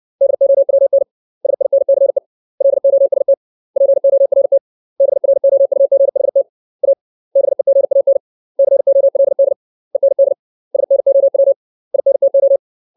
Morse code, heard as a keyed tone, about 60 wpm, 550 Hz. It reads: DOWN SINCE LOST PORT BROUGHT A BEGAN COLD END SNOW INTO
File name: 632 words / 632 Words 60WPM